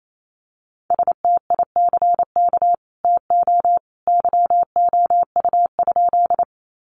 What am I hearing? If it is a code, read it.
Morse code, heard as a keyed tone, about 28 wpm, 700 Hz. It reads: STICK TO YOU?
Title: Sentences from Top 1200 Words 28wpm